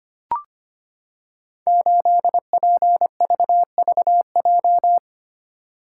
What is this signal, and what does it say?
Morse code, heard as a keyed tone, about 25 wpm, 700 Hz. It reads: 8PVVJ
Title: Code Groups - 5 Letter:Numbers - 25wpm